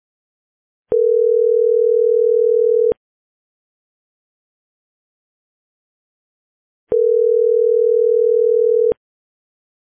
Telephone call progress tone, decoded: ringback tone